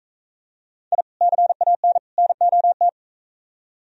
Morse code, heard as a keyed tone, about 45 wpm, 700 Hz. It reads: I CAN NOT